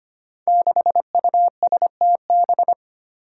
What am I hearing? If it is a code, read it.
Morse code, heard as a keyed tone, about 25 wpm, 700 Hz. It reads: 6USTB
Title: Five Challenging Letter-Number Combinations 25wpm